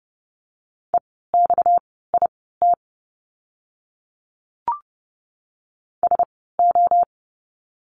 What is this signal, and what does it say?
Morse code, heard as a keyed tone, about 30 wpm, spaced as for 17 wpm, 700 Hz. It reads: EXIT SO
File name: course / 30:10wpm / 230 Lesson 37f - Letter x - Words 30wpm